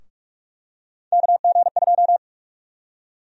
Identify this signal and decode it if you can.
Morse code, heard as a keyed tone, about 46 wpm, 700 Hz. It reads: KG2
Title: USA Call Sign Prefixes - 45wpm